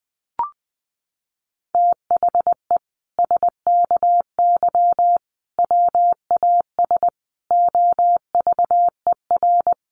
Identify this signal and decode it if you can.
Morse code, heard as a keyed tone, about 20 wpm, 700 Hz. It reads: THE SKY WAS OVER
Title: Sentences from Top 1200 Words 20wpm